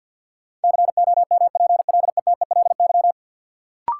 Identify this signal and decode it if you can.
Morse code, heard as a keyed tone, about 52 wpm, 700 Hz. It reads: COMPLETELY
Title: Sentences from Top 1100 Words - New Word Drill Alt - 50wpm